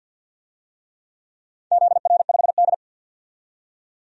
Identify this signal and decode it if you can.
Morse code, heard as a keyed tone, about 48 wpm, 700 Hz. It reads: ZR5D
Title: DXCC-100 Call Signs - 4 Characters - 50wpm